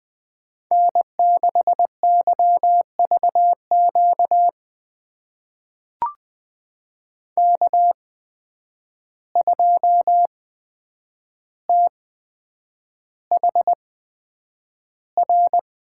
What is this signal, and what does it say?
Morse code, heard as a keyed tone, about 20 wpm, spaced as for 9 wpm, 700 Hz. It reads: N6YVQ K2THR
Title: USA Call Signs - 5 Characters - 8x Character Spacing - 20wpm